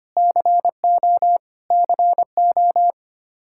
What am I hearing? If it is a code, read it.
Morse code, heard as a keyed tone, about 25 wpm, 700 Hz. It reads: CO CO
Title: K1USN SST QSO Practice 25wpm